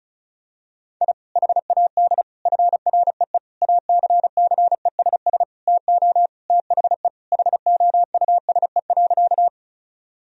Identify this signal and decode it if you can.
Morse code, heard as a keyed tone, about 36 wpm, 700 Hz. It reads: I HAD FREE ACCESS TO THE HOUSE.
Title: Sentences from Top 1000 Words - Review Part 4 - 35wpm